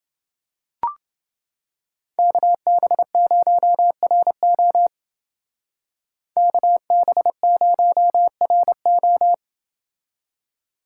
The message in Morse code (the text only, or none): KB0RO KB0RO